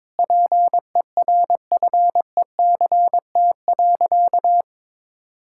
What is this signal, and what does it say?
Morse code, heard as a keyed tone, about 22 wpm, 700 Hz. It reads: PERFECT.
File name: practice / 22wpm / Sentences with 8-Word Phrases - 22wpm